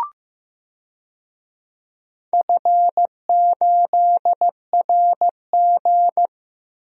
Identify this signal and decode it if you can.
Morse code, heard as a keyed tone, about 16 wpm, 700 Hz. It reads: F8RG
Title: DXCC-100 Call Signs - 4 Characters - 15wpm